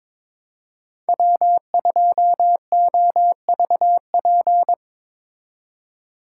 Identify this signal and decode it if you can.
Morse code, heard as a keyed tone, about 23 wpm, 700 Hz. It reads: W2OVP
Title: Speed Racing - USA 1x3 Call Signs 22wpm